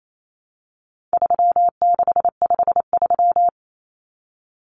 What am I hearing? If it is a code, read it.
Morse code, heard as a keyed tone, about 28 wpm, 700 Hz. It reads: 3653
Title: Common Cut Numbers - 4-Digit 28wpm